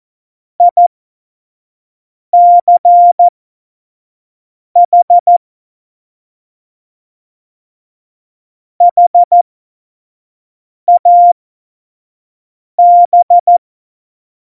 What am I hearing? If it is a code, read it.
Morse code, heard as a keyed tone, about 14 wpm, spaced as for 5 wpm, 700 Hz. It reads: ICH HAB
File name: Morsekurs / Witze / Witz 06. November 2018, 25 F70